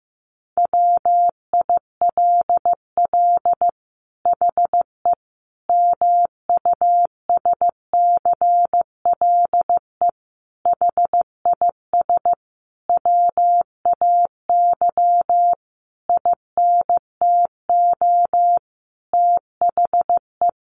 Morse code, heard as a keyed tone, about 15 wpm, 700 Hz. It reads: WILL HE MUSCLE HIS WAY INTO THE
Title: Sentences from Top 1300 Words 15wpm